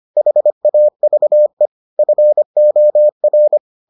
Morse code, heard as a keyed tone, about 25 wpm, 600 Hz. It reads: HAVE FOR